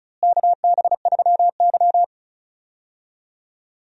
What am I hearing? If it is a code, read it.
Morse code, heard as a keyed tone, about 36 wpm, 700 Hz. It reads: KB3Y
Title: USA Call Signs - 4 Characters - Repeated 3x - 35wpm